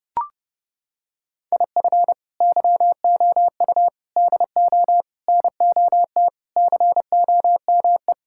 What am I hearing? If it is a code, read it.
Morse code, heard as a keyed tone, about 30 wpm, 700 Hz. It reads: IF YOU DO NOT COME
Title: Sentences from Top 1000 Words - Review Part 5 - 30wpm